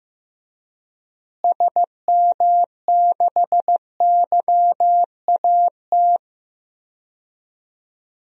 Morse code, heard as a keyed tone, about 15 wpm, 700 Hz. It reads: SM6YAT
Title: DXCC-100 Call Signs - 6 Characters - Repeated 2x - 15wpm